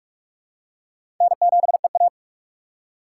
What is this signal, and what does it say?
Morse code, heard as a keyed tone, about 44 wpm, 690 Hz. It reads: N7EA
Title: USA Call Signs - 4 Characters - 2x Character Spacing - 45wpm